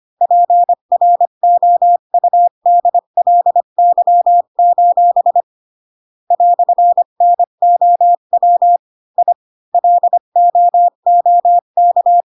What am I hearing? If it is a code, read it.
Morse code, heard as a keyed tone, about 25 wpm, 700 Hz. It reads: PROUDLY: "NOW I LOOK
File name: Pinocchio / Pinocchio ch8 20 25 ik1hge